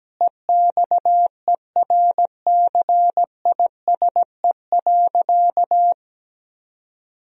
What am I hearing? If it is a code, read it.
Morse code, heard as a keyed tone, about 17 wpm, 700 Hz. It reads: EXERCISE.
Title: Sentences from Top 1000 Words - Review Part 4 - 17wpm